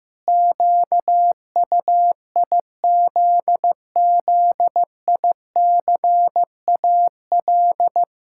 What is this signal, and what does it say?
Morse code, heard as a keyed tone, about 15 wpm, 700 Hz. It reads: QUIZZICAL
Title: Pangram Challenge v2 - 15wpm